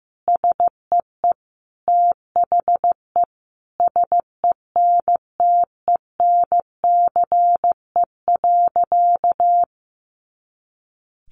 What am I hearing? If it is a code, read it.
Morse code, heard as a keyed tone, about 15 wpm, 700 Hz. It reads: SEE THE SENTENCE.
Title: Additional Easy Phrases 15wpm